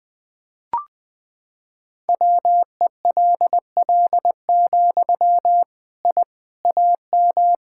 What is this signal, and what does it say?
Morse code, heard as a keyed tone, about 20 wpm, 700 Hz. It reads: WELL, I AM
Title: Sentences from Top 1000 Words - Review Part 5 - 20wpm